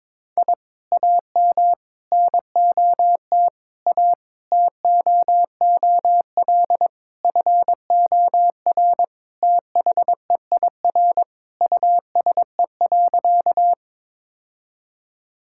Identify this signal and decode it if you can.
Morse code, heard as a keyed tone, about 21 wpm, 700 Hz. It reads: I AM NOT A TOOL FOR THEIR USE.
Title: Sentences from Top 1000 Words - Review Part 6 - 22wpm